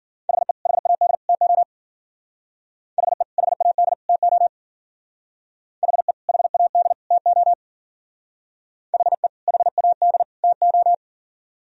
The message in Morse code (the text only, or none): HE HAD TO HE HAD TO HE HAD TO HE HAD TO